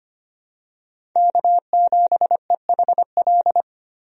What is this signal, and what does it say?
Morse code, heard as a keyed tone, about 25 wpm, 700 Hz. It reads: K7EHL
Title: USA Call Signs - 5 Characters - 25wpm